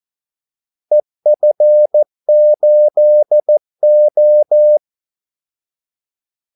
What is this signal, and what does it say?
Morse code, heard as a keyed tone, about 14 wpm, 600 Hz. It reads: EF8O
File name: Callsigns-14wpm